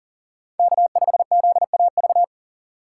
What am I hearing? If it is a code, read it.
Morse code, heard as a keyed tone, about 37 wpm, 700 Hz. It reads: K5ZAV
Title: USA Call Signs - 5 Characters - 6x Character Spacing - 40wpm